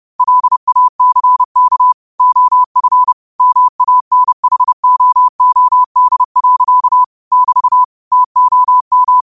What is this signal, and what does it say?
Morse code, heard as a keyed tone, about 30 wpm, 1000 Hz. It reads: LACK OF MANHOOD. <BT> TOM